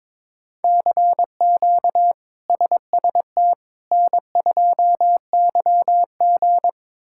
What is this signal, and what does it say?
Morse code, heard as a keyed tone, about 22 wpm, 700 Hz. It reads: CQ SST N2YG
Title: K1USN SST QSO Practice - 22wpm